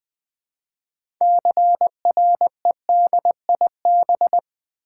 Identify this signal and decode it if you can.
Morse code, heard as a keyed tone, about 20 wpm, 700 Hz. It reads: CREDIB
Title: Build A Word - 3 Letters At A Time - 8-Letter Words - 20wpm